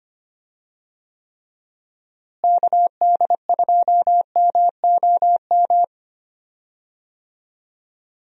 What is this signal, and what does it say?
Morse code, heard as a keyed tone, about 25 wpm, 700 Hz. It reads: KD2MOM
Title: USA Call Signs - 6 Characters - Repeated 3x - 25wpm